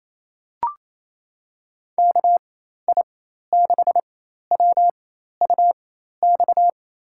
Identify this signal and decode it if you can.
Morse code, heard as a keyed tone, about 28 wpm, spaced as for 13 wpm, 700 Hz. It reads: KI6WUX
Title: USA Call Signs - 4x Character Spacing - 28wpm